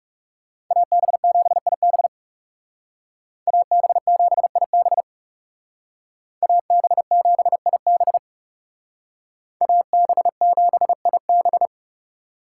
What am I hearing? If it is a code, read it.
Morse code, heard as a keyed tone, about 40 wpm, 700 Hz. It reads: AB7IB AB7IB AB7IB AB7IB